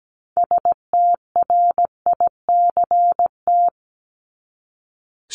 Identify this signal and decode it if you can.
Morse code, heard as a keyed tone, about 17 wpm, 700 Hz. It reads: STRICT
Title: Top 1501-1750 Words 17wpm